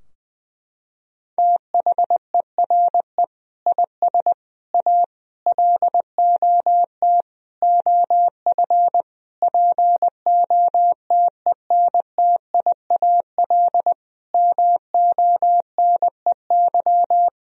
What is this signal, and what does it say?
Morse code, heard as a keyed tone, about 20 wpm, 700 Hz. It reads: THERE IS A LOT OF POTENTIAL MONEY
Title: Sentences from Top 1000 Words - Review Part 4 - 20wpm